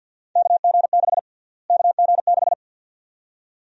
Morse code, heard as a keyed tone, about 52 wpm, 700 Hz. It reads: KG6 KG6
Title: USA Call Sign Prefixes - Repeated 2x - 50wpm